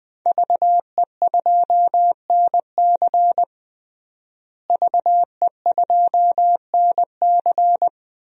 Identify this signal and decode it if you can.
Morse code, heard as a keyed tone, about 20 wpm, 700 Hz. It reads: VE2NC VE2NC
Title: DXCC-100 Call Signs - 5 Characters - Repeated 2x - 20wpm